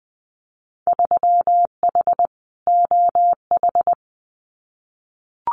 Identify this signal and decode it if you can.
Morse code, heard as a keyed tone, about 19 wpm, 700 Hz. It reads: 3H OH
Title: Winter Field Day Exchanges Encore 20wpm